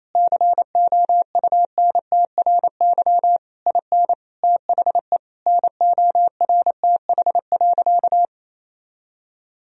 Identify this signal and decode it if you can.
Morse code, heard as a keyed tone, about 28 wpm, 700 Hz. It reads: COUNTRY IN THE NORTH.
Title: Sentences from Top 1000 Words - Review Part 1 - 28wpm